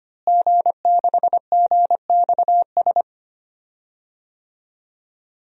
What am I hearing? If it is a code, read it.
Morse code, heard as a keyed tone, about 25 wpm, 700 Hz. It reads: G6GXS
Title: DXCC-100 Call Signs - 5 Characters - Repeated 2x - 25wpm